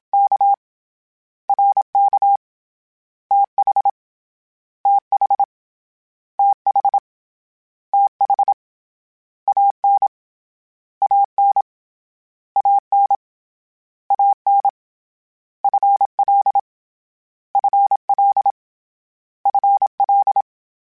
Morse code, heard as a keyed tone, about 27 wpm, 800 Hz. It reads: K R K T H T H T H T H A N A N A N A N F L F L F L
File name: Contrasts / Contrasts-mp3s / 2LtrContrasts @25